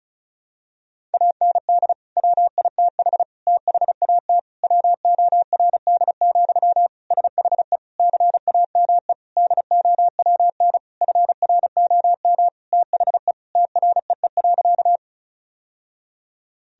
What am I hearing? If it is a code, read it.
Morse code, heard as a keyed tone, about 36 wpm, 700 Hz. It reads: AND WITH THAT WORD, SHE CAME DOWN FROM THE TREE.
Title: Sentences from Top 1000 Words - Review Part 1 - 35wpm